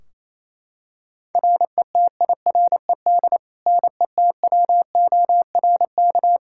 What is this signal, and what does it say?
Morse code, heard as a keyed tone, about 28 wpm, 700 Hz. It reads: RETIRED NETWORK
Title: Practice QSO - Exchange 3 - More Info - 28wpm